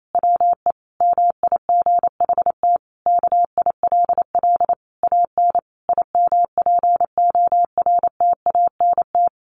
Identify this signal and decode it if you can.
Morse code, heard as a keyed tone, about 28 wpm, 700 Hz. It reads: WE MIGHT KILL AN IMPORTANT
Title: A Sound Of Thunder 28wpm